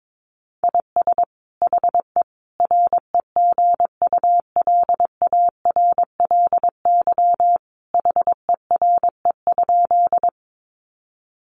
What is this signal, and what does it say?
Morse code, heard as a keyed tone, about 22 wpm, 700 Hz. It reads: IS HE REGULARLY HERE?
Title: Sentences from Top 1400 Words 22wpm